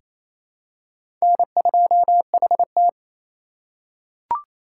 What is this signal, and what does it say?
Morse code, heard as a keyed tone, about 28 wpm, 700 Hz. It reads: N2HT
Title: USA Call Signs - 4 Characters - 3x Character Spacing - 28wpm